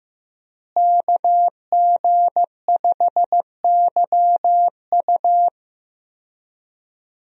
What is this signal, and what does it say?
Morse code, heard as a keyed tone, about 15 wpm, 700 Hz. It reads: KG5YU
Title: USA Call Signs - 5 Characters - Repeated 2x - 15wpm